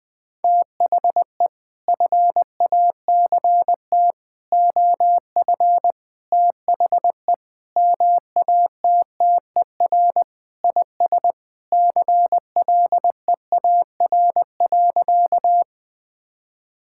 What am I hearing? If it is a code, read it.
Morse code, heard as a keyed tone, about 21 wpm, 700 Hz. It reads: THE FACT OF THE MATTER IS CLEAR.
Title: Sentences with 6-Word Phrases - Spoken Beginning - 20wpm